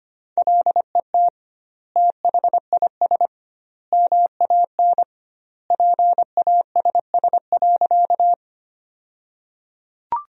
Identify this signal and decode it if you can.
Morse code, heard as a keyed tone, about 25 wpm, 700 Hz. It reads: LET THIS MAN PASS.
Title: 2x Word-Spacing - Sentences from Top 400 Words - 25wpm